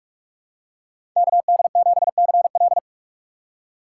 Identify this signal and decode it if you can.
Morse code, heard as a keyed tone, about 47 wpm, 700 Hz. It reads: KD7CL
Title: USA Call Signs - 5 Characters - 3x Character Spacing - 45wpm